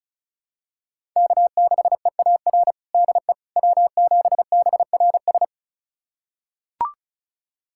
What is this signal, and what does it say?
Morse code, heard as a keyed tone, about 36 wpm, 700 Hz. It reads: K6EAR DE W7BRS
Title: USA Callsign DE Callsign - 5 Characters - 35wpm